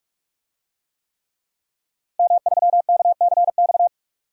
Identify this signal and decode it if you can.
Morse code, heard as a keyed tone, about 45 wpm, 700 Hz. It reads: M3KCX
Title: DXCC-100 Call Signs - 5 Characters - 45wpm